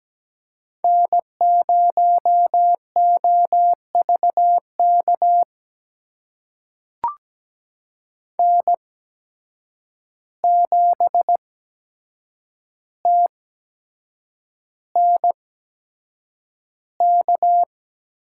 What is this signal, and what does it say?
Morse code, heard as a keyed tone, about 17 wpm, spaced as for 7 wpm, 700 Hz. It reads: N0OVK N7TNK
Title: USA Call Signs - 5 Characters - 8x Character Spacing - 17wpm